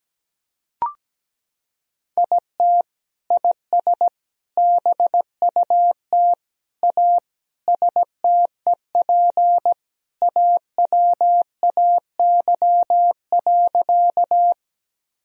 Morse code, heard as a keyed tone, about 17 wpm, 700 Hz. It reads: IT IS BUT A STEP AWAY.
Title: Sentences from Top 1000 Words - Review Part 2 - 17wpm